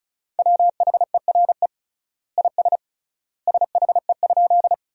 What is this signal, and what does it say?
Morse code, heard as a keyed tone, about 36 wpm, 700 Hz. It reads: WHERE IS SHE?
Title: 3x Word-Spacing - Sentences from Top 200 Words - 35wpm